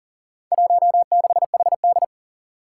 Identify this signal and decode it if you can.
Morse code, heard as a keyed tone, about 40 wpm, 700 Hz. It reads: 16HD
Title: Four Challenging Letter-Number Combinations 40wpm